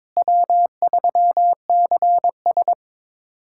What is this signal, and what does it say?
Morse code, heard as a keyed tone, about 22 wpm, 700 Hz. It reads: W3CS
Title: USA Callsign DE Callsign - 4 Characters - 22wpm